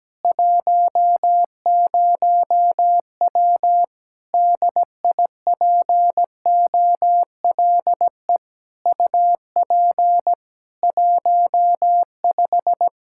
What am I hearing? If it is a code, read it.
Morse code, heard as a keyed tone, about 17 wpm, 700 Hz. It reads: 10W DIPOLE UP 15
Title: Practice QSO - Exchange 2 - Info - 17wpm